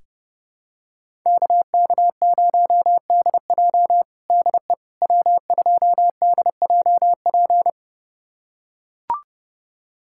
Morse code, heard as a keyed tone, about 30 wpm, 700 Hz. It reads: KK0DJ DE W2DJP
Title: USA Callsign DE Callsign - 5 Characters - 30wpm